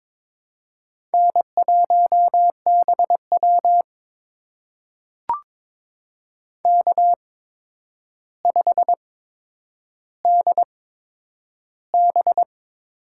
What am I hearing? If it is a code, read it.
Morse code, heard as a keyed tone, about 22 wpm, spaced as for 9 wpm, 700 Hz. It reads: N1BW K5DB
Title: USA Call Signs - 4 Characters - 8x Character Spacing - 22wpm